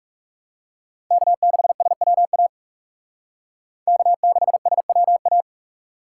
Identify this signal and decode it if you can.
Morse code, heard as a keyed tone, about 36 wpm, 700 Hz. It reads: K6SWA K6SWA
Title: Speed Racing - USA 1x3 Call Signs 30wpm